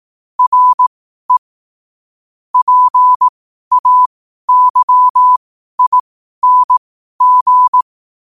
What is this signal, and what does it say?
Morse code, heard as a keyed tone, about 18 wpm, spaced as for 12 wpm, 1000 Hz. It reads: RE PAYING
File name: qotdfiles / 12wpm